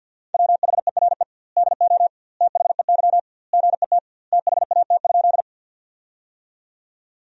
Morse code, heard as a keyed tone, about 52 wpm, 700 Hz. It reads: WHERE DO THEY GET THAT?
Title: 2x Word-Spacing - Sentences from Top 200 Words - 50wpm